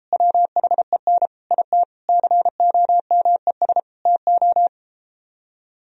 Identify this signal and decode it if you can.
Morse code, heard as a keyed tone, about 33 wpm, 700 Hz. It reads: WHEN IT COMES TO